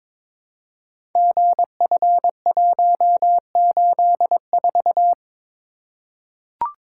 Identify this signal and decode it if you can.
Morse code, heard as a keyed tone, about 22 wpm, 700 Hz. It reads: GF184
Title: Code Groups - 5 Letter:Numbers - 22wpm